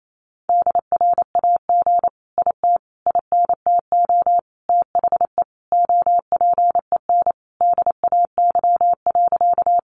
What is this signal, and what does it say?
Morse code, heard as a keyed tone, about 28 wpm, 700 Hz. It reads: DRAG IT INTO THE OPEN DAY.
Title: Sentences from Top 1500 Words 28wpm